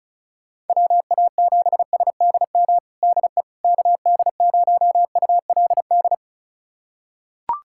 Morse code, heard as a keyed tone, about 36 wpm, 700 Hz. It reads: WA7SDM DE KD0ULD
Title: USA Callsign DE Callsign - 6 Characters - 35wpm